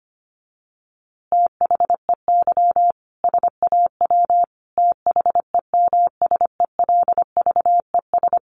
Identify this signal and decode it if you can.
Morse code, heard as a keyed tone, about 25 wpm, 700 Hz. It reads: THEY SAW THEMSELVES
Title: A Sound Of Thunder 25wpm